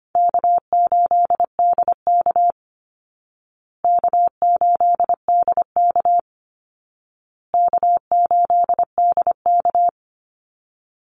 Morse code, heard as a keyed tone, about 25 wpm, 700 Hz. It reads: K8DK K8DK K8DK
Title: Call Signs Repeated 2x - 25wpm